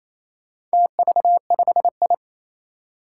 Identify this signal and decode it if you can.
Morse code, heard as a keyed tone, about 27 wpm, 700 Hz. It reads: TV5I
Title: Four Challenging Letter-Number Combinations 28wpm